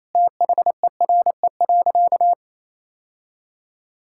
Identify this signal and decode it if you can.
Morse code, heard as a keyed tone, about 28 wpm, 700 Hz. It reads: THERE.
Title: Sentences from Top 1000 Words - Review Part 6 - 28wpm